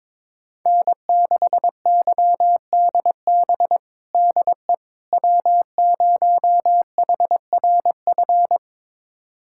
Morse code, heard as a keyed tone, about 22 wpm, 700 Hz. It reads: N6YDB DE W0HRF
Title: USA Callsign DE Callsign - 5 Characters - 22wpm